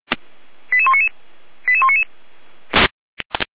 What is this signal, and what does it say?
Five-tone selective call (ZVEI), heard as quick runs of five tones, tones of about 70 ms. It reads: 88199 88199